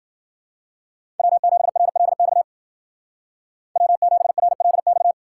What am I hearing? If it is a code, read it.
Morse code, heard as a keyed tone, about 49 wpm, 700 Hz. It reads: W7RLX W7RLX
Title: Speed Racing - USA Call Signs 40wpm